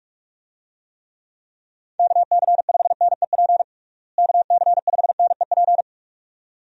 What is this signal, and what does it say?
Morse code, heard as a keyed tone, about 46 wpm, 700 Hz. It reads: KC5NEP KC5NEP
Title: USA Call Signs - 6 Characters - Repeated 2x - 45wpm